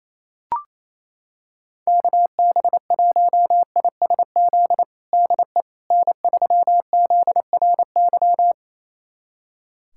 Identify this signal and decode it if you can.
Morse code, heard as a keyed tone, about 28 wpm, 700 Hz. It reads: KB1ISZ DE N3ZRY
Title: USA Callsign DE Callsign 28wpm